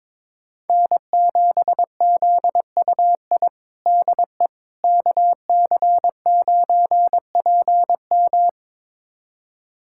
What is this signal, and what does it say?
Morse code, heard as a keyed tone, about 22 wpm, 700 Hz. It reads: N7ZUI DE KC9PM
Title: USA Callsign DE Callsign - 5 Characters - 22wpm